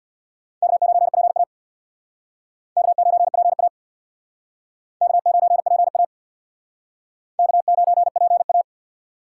Morse code, heard as a keyed tone, about 64 wpm, 700 Hz. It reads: K9PA K9PA K9PA K9PA